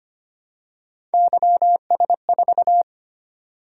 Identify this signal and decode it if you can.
Morse code, heard as a keyed tone, about 25 wpm, 700 Hz. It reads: YS4
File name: Three Letter-Number Combinations - 25wpm